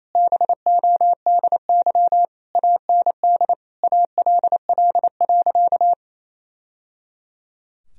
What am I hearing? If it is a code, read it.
Morse code, heard as a keyed tone, about 28 wpm, 700 Hz. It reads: BODY AND ALL.
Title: Sentences from Top 1000 Words - Review Part 2 - 28wpm